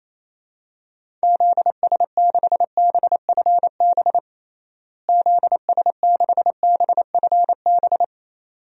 Morse code, heard as a keyed tone, about 28 wpm, 700 Hz. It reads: ZS6BFB ZS6BFB
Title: DXCC-100 Call Signs - 6 Characters - Repeated 2x - 28wpm